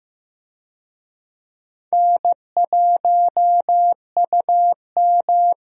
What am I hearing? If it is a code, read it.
Morse code, heard as a keyed tone, about 15 wpm, 700 Hz. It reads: N1UM
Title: USA Call Signs - 4 Characters - Repeated 2x - 15wpm